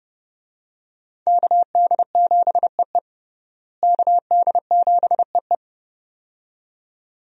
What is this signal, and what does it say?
Morse code, heard as a keyed tone, about 30 wpm, 700 Hz. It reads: KD7EE KD7EE
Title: USA Call Signs - 5 Characters - Repeated 2x - 30wpm